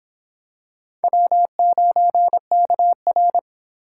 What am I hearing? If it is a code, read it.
Morse code, heard as a keyed tone, about 26 wpm, 700 Hz. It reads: W9KR